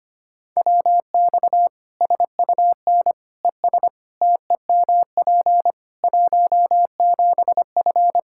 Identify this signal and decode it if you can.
Morse code, heard as a keyed tone, about 25 wpm, 700 Hz. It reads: WX SUN ES TEMP 17F
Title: Practice QSO - Exchange 2 - Info - 25wpm